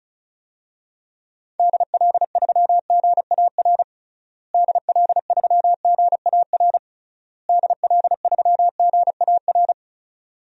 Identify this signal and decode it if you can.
Morse code, heard as a keyed tone, about 36 wpm, 700 Hz. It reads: DL3GAR DL3GAR DL3GAR
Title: DXCC-100 Call Signs - 6 Characters - Repeated 3x - 35wpm